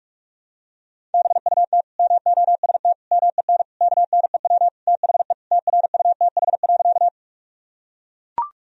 Morse code, heard as a keyed tone, about 42 wpm, 700 Hz. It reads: BUT MOST MEN KNEW THE TRUTH.
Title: Sentences from Top 1000 Words - Review Part 1 - 45wpm